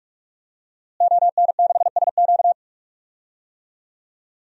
Morse code, heard as a keyed tone, about 46 wpm, 700 Hz. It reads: ON6SQ
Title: DXCC-100 Call Signs - 5 Characters - Repeated 3x - 45wpm